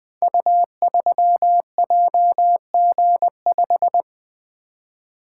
Morse code, heard as a keyed tone, about 20 wpm, 700 Hz. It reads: U3JG5
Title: Code Groups - 5 Letter:Numbers - 20wpm